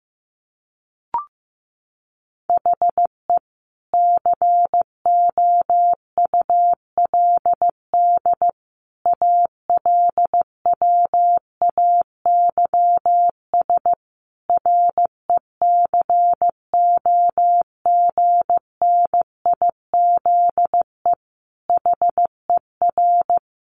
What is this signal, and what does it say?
Morse code, heard as a keyed tone, about 15 wpm, 700 Hz. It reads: HE COULD ALWAYS RECOGNIZE HER
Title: Sentences from Top 1400 Words 15wpm